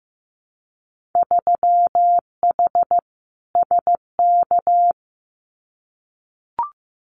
Morse code, heard as a keyed tone, about 15 wpm, 700 Hz. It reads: 3H SK
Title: Winter Field Day Exchanges 15wpm